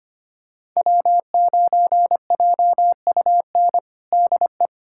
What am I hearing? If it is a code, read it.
Morse code, heard as a keyed tone, about 25 wpm, 700 Hz. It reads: W9JUN DE
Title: USA Callsign DE Callsign - 5 Characters - 25wpm